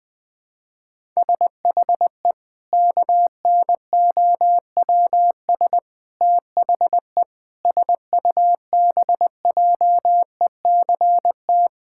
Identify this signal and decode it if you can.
Morse code, heard as a keyed tone, about 20 wpm, 700 Hz. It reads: SHE KNOWS THE SUBJECT
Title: Sentences with 4-Word Phrases - Spoken Beginning - 20wpm